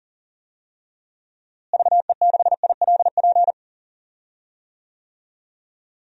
Morse code, heard as a keyed tone, about 40 wpm, 700 Hz. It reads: VE6ILP
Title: DXCC-100 Call Signs - 6 Characters - Repeated 2x - 40wpm